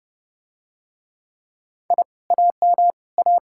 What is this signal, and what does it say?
Morse code, heard as a keyed tone, about 30 wpm, 700 Hz. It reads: I AM A